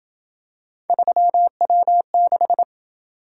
Morse code, heard as a keyed tone, about 26 wpm, 700 Hz. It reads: 3W6